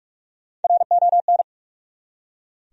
Morse code, heard as a keyed tone, about 44 wpm, 700 Hz. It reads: RON
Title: USA Common Names - Max 4 Letters - 45wpm